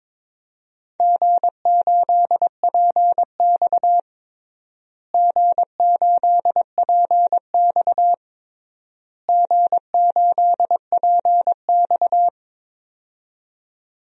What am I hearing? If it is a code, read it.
Morse code, heard as a keyed tone, about 22 wpm, 700 Hz. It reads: G8PX G8PX G8PX
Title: DXCC-100 Call Signs - 4 Characters - Repeated 3x - 22wpm